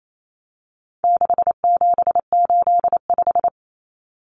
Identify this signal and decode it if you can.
Morse code, heard as a keyed tone, about 28 wpm, 700 Hz. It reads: 6785
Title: Common Cut Numbers - 4-Digit 28wpm